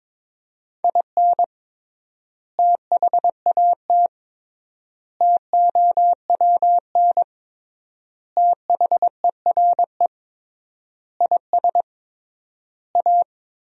Morse code, heard as a keyed tone, about 21 wpm, 700 Hz. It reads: IN THAT TOWN THERE IS A
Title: 3x Word-Spacing - Sentences from Top 500 Words - 22wpm